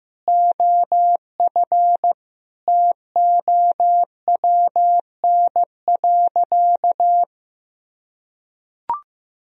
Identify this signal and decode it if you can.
Morse code, heard as a keyed tone, about 15 wpm, 700 Hz. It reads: OF TOWN.
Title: Sentences from Top 1000 Words - Review Part 1 - 15wpm